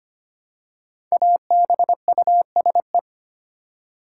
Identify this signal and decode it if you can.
Morse code, heard as a keyed tone, about 25 wpm, 700 Hz. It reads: ABUSE
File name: 5 Letter Words - First 3 Letters Given - 25wpm